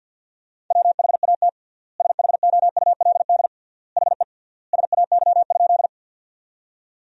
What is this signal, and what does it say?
Morse code, heard as a keyed tone, about 47 wpm, 700 Hz. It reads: WHAT SHOULD HE SAY?
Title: 3x Word-Spacing - Sentences from Top 300 Words - 50wpm